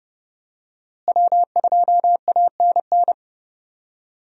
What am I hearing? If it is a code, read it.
Morse code, heard as a keyed tone, about 29 wpm, 700 Hz. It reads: W2ANN
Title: USA Call Signs - 5 Characters - Repeated 3x - 30wpm